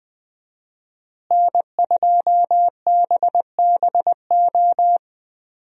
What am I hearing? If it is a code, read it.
Morse code, heard as a keyed tone, about 20 wpm, 700 Hz. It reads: N2BBO
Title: USA Call Signs - 5 Characters - Repeated 3x - 20wpm